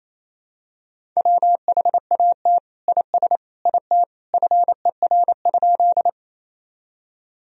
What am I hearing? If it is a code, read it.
Morse code, heard as a keyed tone, about 28 wpm, 700 Hz. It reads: WHAT IS IT FER?
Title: Sentences from Top 100 Words and Abbreviations 28wpm